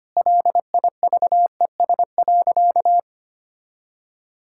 Morse code, heard as a keyed tone, about 25 wpm, 700 Hz. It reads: LIVES.
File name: practice / 25wpm / Sentences with 2-Word Phrases - 25wpm